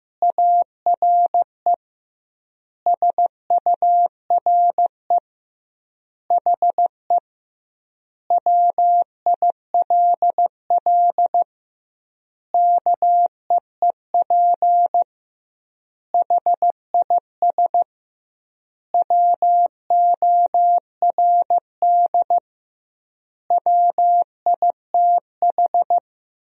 Morse code, heard as a keyed tone, about 16 wpm, 700 Hz. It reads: ARE SURE HE WILL KEEP HIS WORD WITH
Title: 2x Word-Spacing - Sentences from Top 300 Words - 15wpm